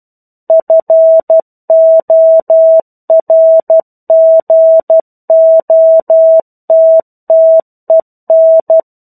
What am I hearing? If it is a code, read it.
Morse code, heard as a keyed tone, about 12 wpm, 650 Hz. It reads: FORGOTTEN
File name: 01-Chapter-01 12 WPM